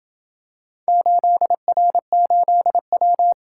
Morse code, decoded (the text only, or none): 8R8W